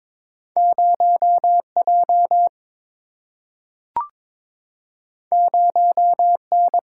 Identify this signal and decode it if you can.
Morse code, heard as a keyed tone, about 21 wpm, 700 Hz. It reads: 0 J 0 N
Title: Single-Letter Single-Number 22wpm